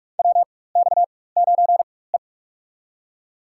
Morse code, heard as a keyed tone, about 42 wpm, spaced as for 20 wpm, 700 Hz. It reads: WX9E
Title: USA Call Signs - 4x Character Spacing - 45wpm